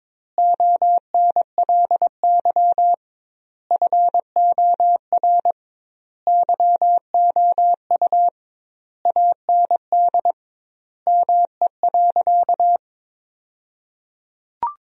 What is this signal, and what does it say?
Morse code, heard as a keyed tone, about 22 wpm, 700 Hz. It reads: ONLY FOR YOU AND ME.
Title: 2x Word-Spacing - Sentences from Top 500 Words - 22wpm